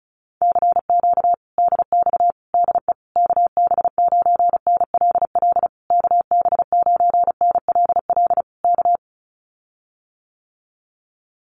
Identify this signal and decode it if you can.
Morse code, heard as a keyed tone, about 34 wpm, 700 Hz. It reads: CQ DX DE KB9NLL KB9NLL K
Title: Common QSO Phrases Encore 35wpm